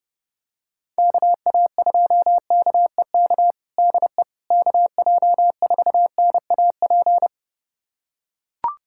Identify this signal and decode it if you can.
Morse code, heard as a keyed tone, about 29 wpm, 700 Hz. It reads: KA2KEK DE KJ4NAP
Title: USA Callsign DE Callsign - 6 Characters - 30wpm